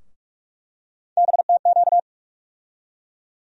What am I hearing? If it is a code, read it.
Morse code, heard as a keyed tone, about 43 wpm, 700 Hz. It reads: BTQ
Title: USA Call Sign Postfixes - 3 Letters 45wpm